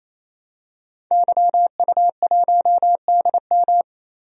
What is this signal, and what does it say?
Morse code, heard as a keyed tone, about 27 wpm, 700 Hz. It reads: YU1DM
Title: DXCC-100 Call Signs - 5 Characters - Repeated 3x - 28wpm